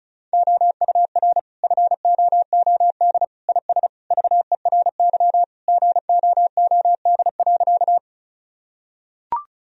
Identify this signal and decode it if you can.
Morse code, heard as a keyed tone, about 36 wpm, 700 Hz. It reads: OUR FOOD IS VERY GOOD.
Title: Sentences from Top 1000 Words - Review Part 1 - 35wpm